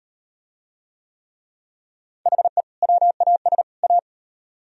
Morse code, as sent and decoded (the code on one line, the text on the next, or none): .... / .-- .- ... / .-
HE WAS A